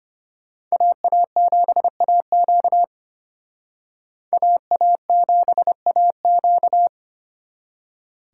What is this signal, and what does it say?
Morse code, heard as a keyed tone, about 32 wpm, 700 Hz. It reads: AA7AQ AA7AQ